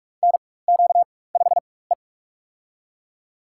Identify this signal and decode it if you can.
Morse code, heard as a keyed tone, about 41 wpm, spaced as for 20 wpm, 700 Hz. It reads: NQ5E
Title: USA Call Signs - 4 Characters - 4x Character Spacing - 45wpm